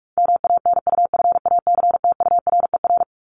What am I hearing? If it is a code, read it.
Morse code, heard as a keyed tone, about 44 wpm, 700 Hz. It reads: MANUFACTURER
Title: Top 1751-2000 Words 45wpm